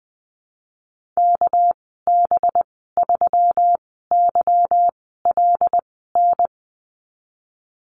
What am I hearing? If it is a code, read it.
Morse code, heard as a keyed tone, about 20 wpm, spaced as for 15 wpm, 700 Hz. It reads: KB3YLN
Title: 238 Lesson 38g - Letter q - Call Signs 20wpm